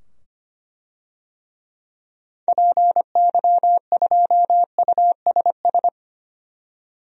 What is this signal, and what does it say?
Morse code, heard as a keyed tone, about 25 wpm, 700 Hz. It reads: PY2USS
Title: DXCC-100 Call Signs - 6 Characters - 25wpm